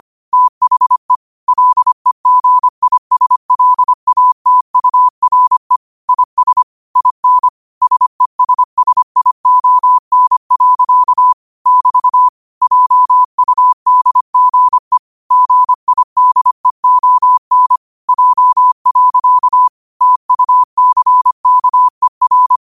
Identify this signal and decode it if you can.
Morse code, heard as a keyed tone, about 25 wpm, 1000 Hz. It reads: THE LEGISLATURE IS IN SESSION. <BT> JUDGE GIDEON J. TUCKER